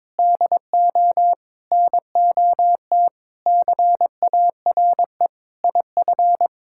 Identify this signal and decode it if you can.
Morse code, heard as a keyed tone, about 22 wpm, 700 Hz. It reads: DO NOT CARE IF
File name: practice / 22wpm / Build A Phrase - 5-Word Phrases - 22wpm